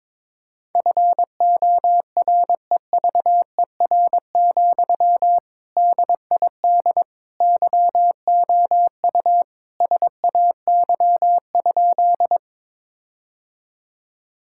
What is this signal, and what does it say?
Morse code, heard as a keyed tone, about 22 wpm, 700 Hz. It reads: FOREVER, DID YOU SAY?
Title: Sentences from Top 1600 Words 22wpm